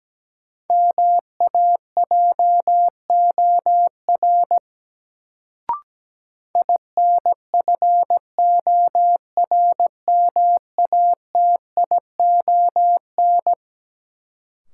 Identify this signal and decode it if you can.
Morse code, heard as a keyed tone, about 17 wpm, 700 Hz. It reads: MAJOR INFORMATION
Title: Sentences from Top 600 Words - New Word Drill - 17wpm